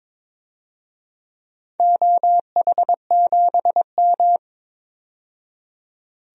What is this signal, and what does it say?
Morse code, heard as a keyed tone, about 22 wpm, 700 Hz. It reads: OH7M
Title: DXCC-100 Call Signs - 4 Characters - 22wpm